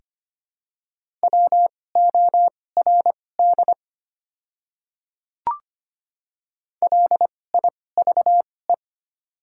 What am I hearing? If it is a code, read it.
Morse code, heard as a keyed tone, about 25 wpm, spaced as for 18 wpm, 700 Hz. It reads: WORD LIVE
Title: 2x Character Spacing - Top 500 Words - 25wpm